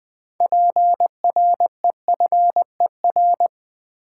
Morse code, heard as a keyed tone, about 20 wpm, 700 Hz. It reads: PREFER
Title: Six Letter Words - 20wpm